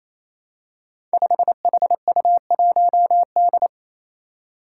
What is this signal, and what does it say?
Morse code, heard as a keyed tone, about 28 wpm, 700 Hz. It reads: 5HU1D
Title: Five Challenging Letter-Number Combinations 28wpm